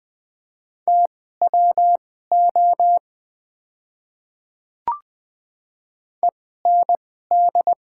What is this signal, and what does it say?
Morse code, heard as a keyed tone, about 20 wpm, spaced as for 15 wpm, 700 Hz. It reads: TWO END
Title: 2x Character Spacing - Top 300 Words - 20wpm